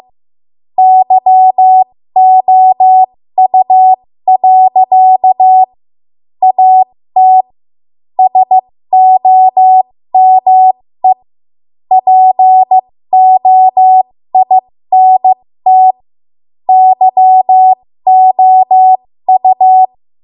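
Morse code, heard as a keyed tone, about 15 wpm, spaced as for 13 wpm, 750 Hz. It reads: YOU. AT SOME POINT YOU
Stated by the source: 13 wpm Morse code practice archive transmission.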